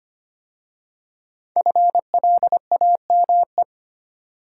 Morse code, heard as a keyed tone, about 25 wpm, 700 Hz. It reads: FLAME
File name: 5 Letter Words - First 3 Letters Given - 25wpm